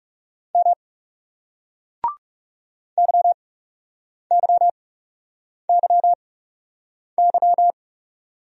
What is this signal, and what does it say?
Morse code, heard as a keyed tone, about 39 wpm, 700 Hz. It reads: M Y Y Y Y